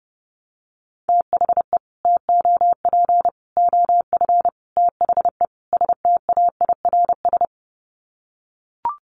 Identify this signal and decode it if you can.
Morse code, heard as a keyed tone, about 30 wpm, 700 Hz. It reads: THE TOP OF THE STAIRS
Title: Sets of 5 Words Encore 30wpm